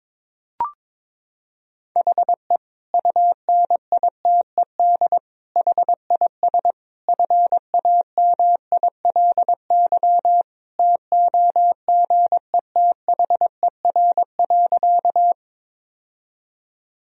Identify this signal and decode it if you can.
Morse code, heard as a keyed tone, about 22 wpm, 700 Hz. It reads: HE UNITED HIS FAMILY TOGETHER.
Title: Sentences from Top 1000 Words - Review Part 4 - 22wpm